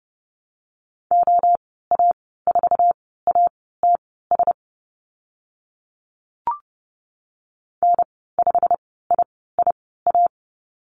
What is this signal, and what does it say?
Morse code, heard as a keyed tone, about 29 wpm, spaced as for 17 wpm, 700 Hz. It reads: OA4ATS N5IIA
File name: Lesson 15g - Number 5 - Call Signs 30wpm